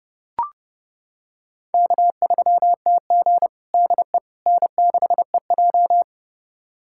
Morse code, heard as a keyed tone, about 30 wpm, 700 Hz. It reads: K3TG DE N6EJ
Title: USA Callsign DE Callsign - 4 Characters - 30wpm